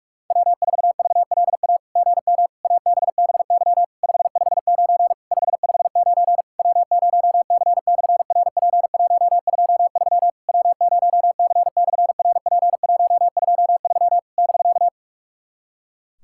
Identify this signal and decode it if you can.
Morse code, heard as a keyed tone, about 41 wpm, 690 Hz. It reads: W4VLA GM ABBY 559 559 W0C/RP123 W0C/RP123 <BK>
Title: SOTA QSO Practice 45wpm